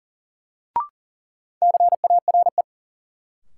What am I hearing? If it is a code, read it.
Morse code, heard as a keyed tone, about 40 wpm, 700 Hz. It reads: CARE
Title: Sets of 3 Words - New Word Drill - 40wpm